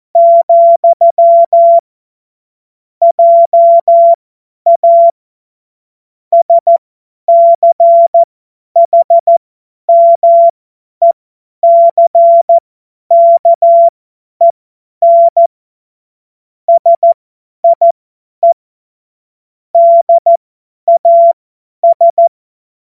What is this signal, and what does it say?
Morse code, heard as a keyed tone, about 14 wpm, spaced as for 10 wpm, 680 Hz. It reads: , JA SCHMECKEN SIE DAS
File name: Morsekurs / Witze / Witz 06. April 2021, 50 F70